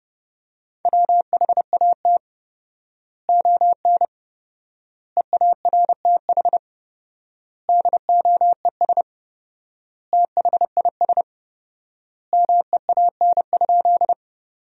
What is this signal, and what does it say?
Morse code, heard as a keyed tone, about 30 wpm, 700 Hz. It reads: WHAT ON EARTH DOES THIS MEAN?
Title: 4x Word-Spacing - Sentences from Top 200 Words - 30wpm